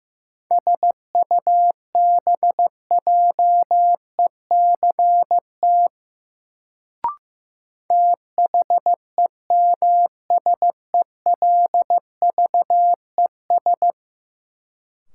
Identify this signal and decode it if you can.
Morse code, heard as a keyed tone, about 15 wpm, 700 Hz. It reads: SUBJECT THEMSELVES
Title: Sentences from Top 800 Words - New Word Drill - 15wpm